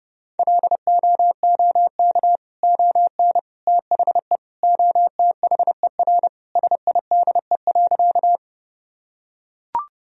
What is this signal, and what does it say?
Morse code, heard as a keyed tone, about 30 wpm, 700 Hz. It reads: LOOK ON THE OTHER SIDE.
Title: Sentences from Top 1000 Words - Review Part 1 - 30wpm